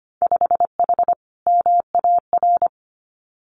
Morse code, heard as a keyed tone, about 25 wpm, 700 Hz. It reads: 5H MAR
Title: Winter Field Day Exchanges Encore 25wpm